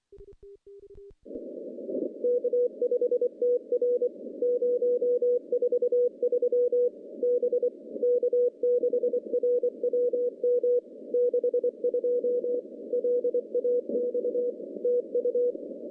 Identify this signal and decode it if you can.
Morse code, heard as a keyed tone, about 24 wpm, 490 Hz. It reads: K5TR 043 B K6RWM 62 LAX TU
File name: contests / CW-SS-k5tr- -0200